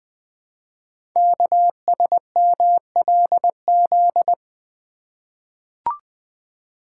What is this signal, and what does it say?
Morse code, heard as a keyed tone, about 20 wpm, 700 Hz. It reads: KSMLZ
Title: Code Groups - 5 Letters - 20wpm